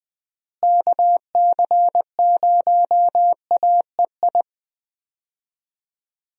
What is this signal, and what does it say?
Morse code, heard as a keyed tone, about 20 wpm, 700 Hz. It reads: KC0AEI
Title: USA Call Signs - 6 Characters - Repeated 2x - 20wpm